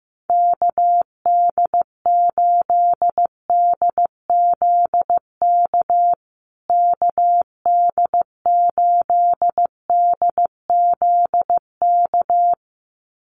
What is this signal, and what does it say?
Morse code, heard as a keyed tone, about 15 wpm, 700 Hz. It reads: KD8DZK KD8DZK
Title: USA Call Signs Repeated 2x 15wpm